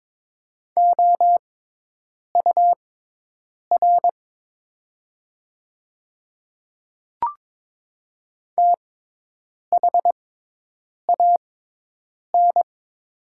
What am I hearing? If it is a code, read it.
Morse code, heard as a keyed tone, about 22 wpm, spaced as for 8 wpm, 700 Hz. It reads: OUR THAN
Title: 6x Character Spacing - Top 200 Words - 22wpm